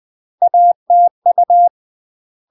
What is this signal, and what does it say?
Morse code, heard as a keyed tone, about 20 wpm, 700 Hz. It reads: ATU
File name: UK MASTER QSOs TRAINING / 20WPM / UK MASTER QSOs 0047